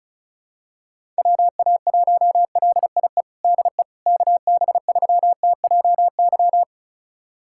Morse code, heard as a keyed tone, about 34 wpm, 700 Hz. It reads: WA1LIE DE KB3TJY
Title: USA Callsign DE Callsign - 6 Characters - 35wpm